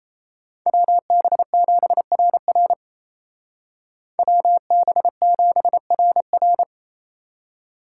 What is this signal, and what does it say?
Morse code, heard as a keyed tone, about 36 wpm, 700 Hz. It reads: WB7RR WB7RR